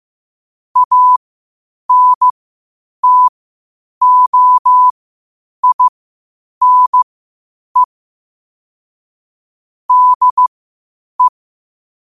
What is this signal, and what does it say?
Morse code, heard as a keyed tone, about 15 wpm, spaced as for 8 wpm, 1000 Hz. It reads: ANTOINE DE
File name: qotdfiles / 8wpm